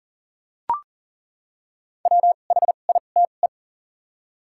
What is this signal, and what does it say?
Morse code, heard as a keyed tone, about 39 wpm, 700 Hz. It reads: W H I T E
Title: 2x Character Spacing - Top 300 Words - 40wpm